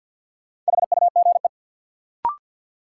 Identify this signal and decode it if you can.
Morse code, heard as a keyed tone, about 53 wpm, 700 Hz. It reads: HUGE
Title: Sentences from Top 1100 Words - New Word Drill - 50wpm